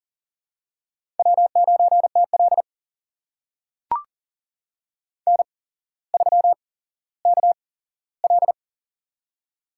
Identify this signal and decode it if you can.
Morse code, heard as a keyed tone, about 43 wpm, spaced as for 17 wpm, 700 Hz. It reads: W9TL N3KL
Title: USA Call Signs - 4 Characters - 8x Character Spacing - 40wpm